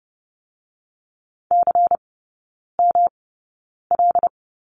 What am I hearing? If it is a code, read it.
Morse code, heard as a keyed tone, about 30 wpm, spaced as for 9 wpm, 700 Hz. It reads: CML
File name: Lesson 18e - Letter m - Three Characters 30wpm